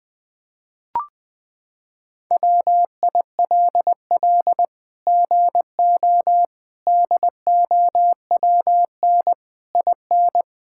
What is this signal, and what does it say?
Morse code, heard as a keyed tone, about 20 wpm, 700 Hz. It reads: WILL GO DOWN IN
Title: Sets of 9 Words - 20wpm